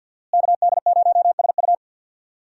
Morse code, heard as a keyed tone, about 50 wpm, 700 Hz. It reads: KD0SU
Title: USA Call Signs - 5 Characters - Repeated 2x - 50wpm